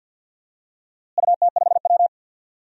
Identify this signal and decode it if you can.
Morse code, heard as a keyed tone, about 50 wpm, 700 Hz. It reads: UT5W
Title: DXCC-100 Call Signs - 4 Characters - Repeated 2x - 50wpm